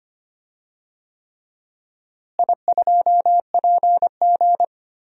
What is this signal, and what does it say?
Morse code, heard as a keyed tone, about 25 wpm, 700 Hz. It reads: I2PG
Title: DXCC-100 Call Signs - 4 Characters - Repeated 3x - 25wpm